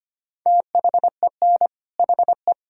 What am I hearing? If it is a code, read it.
Morse code, heard as a keyed tone, about 25 wpm, 700 Hz. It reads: THEN HE
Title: Sentences from Top 1000 Words - Review Part 2 - 25wpm